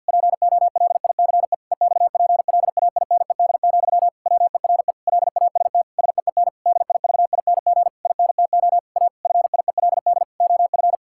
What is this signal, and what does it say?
Morse code, heard as a keyed tone, about 53 wpm, 700 Hz. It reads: POLICE EXPLAINED, WERE LAST SEEN DIVING INTO A FIELD OF
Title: Pangram Challenge v2 - 50wpm